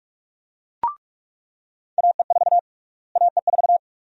Morse code, heard as a keyed tone, about 41 wpm, 700 Hz. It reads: AE4 AE4
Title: USA Call Sign Prefixes - Repeated 2x - 45wpm